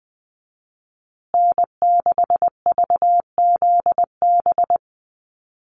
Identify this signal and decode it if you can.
Morse code, heard as a keyed tone, about 20 wpm, 700 Hz. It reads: N6VZB
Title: USA Call Signs Repeated 3x 20wpm